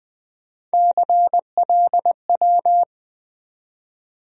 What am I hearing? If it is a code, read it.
Morse code, heard as a keyed tone, about 20 wpm, 700 Hz. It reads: CLW
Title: USA Call Sign Postfixes - 3 Letters 20wpm